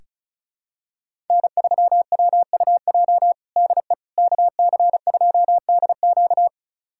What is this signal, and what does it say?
Morse code, heard as a keyed tone, about 36 wpm, 700 Hz. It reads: N3WUJ DE KC2DQ
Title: USA Callsign DE Callsign - 5 Characters - 35wpm